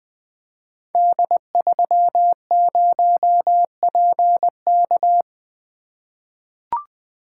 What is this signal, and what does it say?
Morse code, heard as a keyed tone, about 20 wpm, 700 Hz. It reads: D30PK
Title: Code Groups - 5 Letter:Numbers - 20wpm